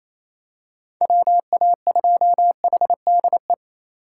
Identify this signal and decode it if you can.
Morse code, heard as a keyed tone, about 28 wpm, 700 Hz. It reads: WA2HDE
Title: USA Call Signs - 6 Characters - 2x Character Spacing - 28wpm